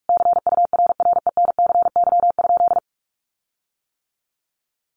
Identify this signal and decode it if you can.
Morse code, heard as a keyed tone, about 44 wpm, 700 Hz. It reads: CURRENCY?
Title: Sentences from Top 1500 Words 45wpm